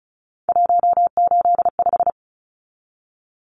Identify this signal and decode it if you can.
Morse code, heard as a keyed tone, about 34 wpm, 700 Hz. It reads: 185
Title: Numbers - 3-Digit 35wpm